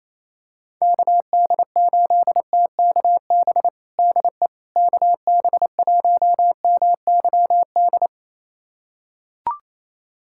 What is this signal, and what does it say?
Morse code, heard as a keyed tone, about 28 wpm, 700 Hz. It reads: KD8TKB DE KB1MYD
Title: USA Callsign DE Callsign - 6 Characters - 28wpm